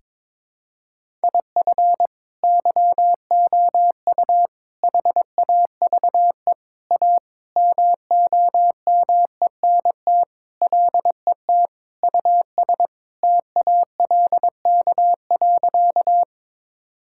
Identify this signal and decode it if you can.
Morse code, heard as a keyed tone, about 22 wpm, 700 Hz. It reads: IF YOU HAVE A MOMENT LET US TALK.
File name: practice / 22wpm / Sentences with 4-Word Phrases - 22wpm